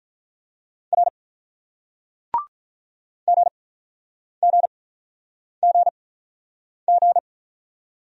Morse code, heard as a keyed tone, about 48 wpm, spaced as for 9 wpm, 700 Hz. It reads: R GGGG